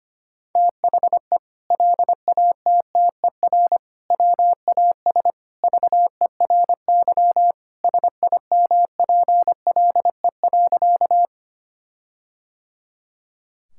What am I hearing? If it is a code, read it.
Morse code, heard as a keyed tone, about 25 wpm, 700 Hz. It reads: THE LATTER WAS VERY SIMPLE.
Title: Sentences from Top 1600 Words 25wpm